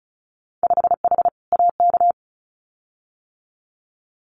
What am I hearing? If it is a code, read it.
Morse code, heard as a keyed tone, about 36 wpm, 700 Hz. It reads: 5H AK
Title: Winter Field Day Exchanges Encore 35wpm